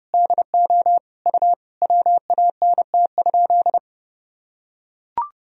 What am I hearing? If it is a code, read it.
Morse code, heard as a keyed tone, about 32 wpm, 700 Hz. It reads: DO U WANT?
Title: Sentences from Top 200 Words and Abbreviations 30wpm